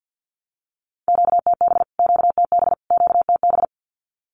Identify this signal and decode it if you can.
Morse code, heard as a keyed tone, about 50 wpm, 700 Hz. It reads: QTB QTB QTB
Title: Q-Codes Repeated 3x 50wpm